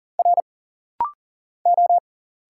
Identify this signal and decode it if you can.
Morse code, heard as a keyed tone, about 40 wpm, 700 Hz. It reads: R O
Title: Single Letters - Rapid-Fire 40wpm